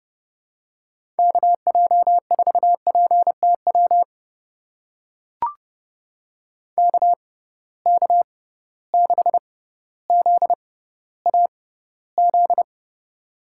Morse code, heard as a keyed tone, about 30 wpm, spaced as for 15 wpm, 700 Hz. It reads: KJ4PTW KK6ZAZ
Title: USA Call Signs - 6 Characters - 6x Character Spacing - 30wpm